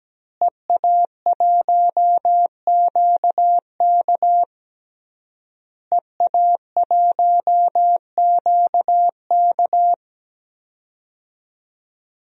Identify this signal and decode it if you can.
Morse code, heard as a keyed tone, about 17 wpm, 700 Hz. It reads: EA1QK EA1QK
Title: DXCC-100 Call Signs - 5 Characters - Repeated 3x - 17wpm